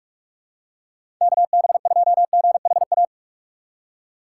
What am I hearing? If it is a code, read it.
Morse code, heard as a keyed tone, about 44 wpm, 700 Hz. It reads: KB2GHA
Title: USA Call Signs - 6 Characters - 6x Character Spacing - 45wpm